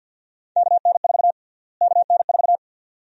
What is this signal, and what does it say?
Morse code, heard as a keyed tone, about 52 wpm, 700 Hz. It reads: KN4 KN4
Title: USA Call Sign Prefixes - Repeated 2x - 50wpm